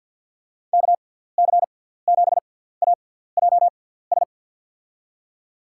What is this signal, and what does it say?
Morse code, heard as a keyed tone, about 48 wpm, spaced as for 17 wpm, 700 Hz. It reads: KC7AJS
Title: USA Call Signs - 6 Characters - 6x Character Spacing - 50wpm